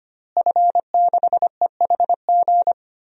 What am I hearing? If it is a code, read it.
Morse code, heard as a keyed tone, about 25 wpm, 700 Hz. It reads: F6EHG
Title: DXCC-100 Call Signs - 5 Characters - Repeated 3x - 25wpm